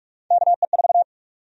Morse code, heard as a keyed tone, about 41 wpm, 700 Hz. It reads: KE4
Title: USA Call Sign Prefixes - Repeated 2x - 45wpm